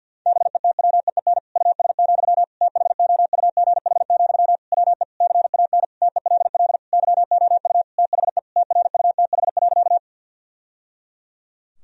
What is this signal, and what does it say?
Morse code, heard as a keyed tone, about 50 wpm, 700 Hz. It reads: BETWEEN US, THOUGH, WE CAN TELL YOU THE TRUTH.
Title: Sentences from Top 1000 Words - Review Part 2 - 50wpm